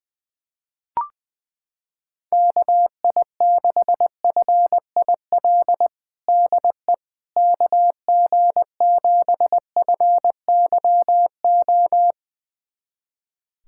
Morse code, heard as a keyed tone, about 20 wpm, 700 Hz. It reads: KI6FIL DE KG7FYO
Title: USA Callsign DE Callsign 20wpm